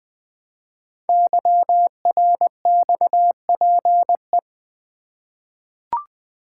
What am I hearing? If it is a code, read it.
Morse code, heard as a keyed tone, about 20 wpm, 700 Hz. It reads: YRXPE
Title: Code Groups - 5 Letters - 20wpm